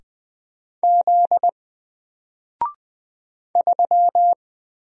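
Morse code, heard as a keyed tone, about 20 wpm, 700 Hz. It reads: Z 3
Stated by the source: Single letter and number Morse drill at 20 wpm, from N4LCD.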